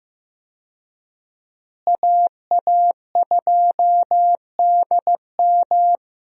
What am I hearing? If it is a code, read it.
Morse code, heard as a keyed tone, about 15 wpm, 700 Hz. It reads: AA2DM
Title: USA Call Signs - 5 Characters - Repeated 2x - 15wpm